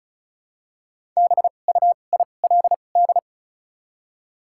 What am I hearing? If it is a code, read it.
Morse code, heard as a keyed tone, about 34 wpm, spaced as for 25 wpm, 700 Hz. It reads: BUILD
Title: 2x Character Spacing - Top 300 Words - 35wpm